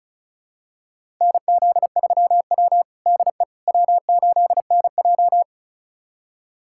Morse code, heard as a keyed tone, about 34 wpm, 700 Hz. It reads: NZ3W DE W8NJ
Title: USA Callsign DE Callsign - 4 Characters - 35wpm